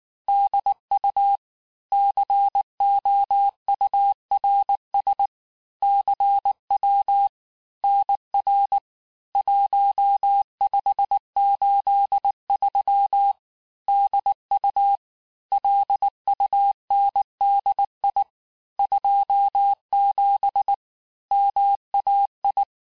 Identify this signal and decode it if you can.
Morse code, heard as a keyed tone, about 19 wpm, 800 Hz. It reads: DU COURS CW NR 1583 DU LUNDI 27 MAI
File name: cours-du-lundi files / cours-1583--texte-chiffre-18-mpm